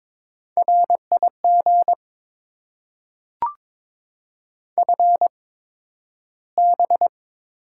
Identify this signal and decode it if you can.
Morse code, heard as a keyed tone, about 22 wpm, spaced as for 12 wpm, 700 Hz. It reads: RIG FB
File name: Top 20 QSO Elements - 8x Character Spacing - 22wpm